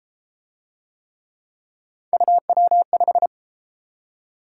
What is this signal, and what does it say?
Morse code, heard as a keyed tone, about 28 wpm, 700 Hz. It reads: UW5